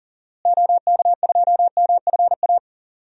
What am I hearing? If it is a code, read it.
Morse code, heard as a keyed tone, about 38 wpm, 700 Hz. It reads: OK2MFA
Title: DXCC-100 Call Signs - 6 Characters - Repeated 3x - 40wpm